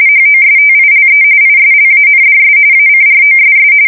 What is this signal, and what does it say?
RTTY, about 45.45 baud. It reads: HOSTILE ISLANDS ACROSS
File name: fsk170 / USS-New-Jersey